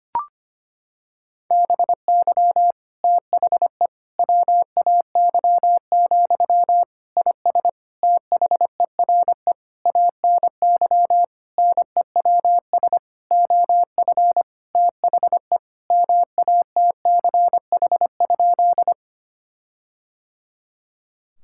Morse code, heard as a keyed tone, about 25 wpm, 700 Hz. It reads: BY THE WAY, IS THERE ANY NEWS OF THE MATCH?
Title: Sentences from Top 1600 Words 25wpm